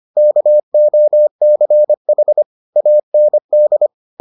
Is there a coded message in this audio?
Yes: Morse code from 0.2 to 3.9 s, heard as a keyed tone, about 25 wpm, 600 Hz. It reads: KOCH AND